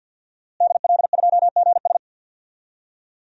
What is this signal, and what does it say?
Morse code, heard as a keyed tone, about 52 wpm, 700 Hz. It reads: DL2GS
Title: DXCC-100 Call Signs - 5 Characters - Repeated 3x - 50wpm